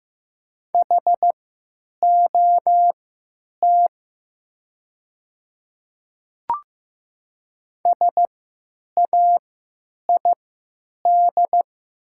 Morse code, heard as a keyed tone, about 15 wpm, spaced as for 9 wpm, 700 Hz. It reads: HOT SAID
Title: 3x Character Spacing - Top 100 Words - 15wpm